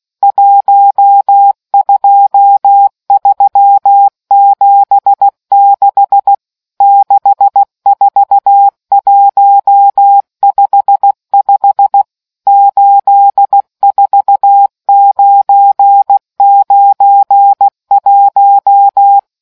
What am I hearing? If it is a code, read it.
Morse code, heard as a keyed tone, about 16 wpm, 800 Hz. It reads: 12376 64155 84991